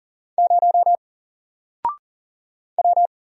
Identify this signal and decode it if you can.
Morse code, heard as a keyed tone, about 40 wpm, 700 Hz. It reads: 0 W
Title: Single Letter-Number 40wpm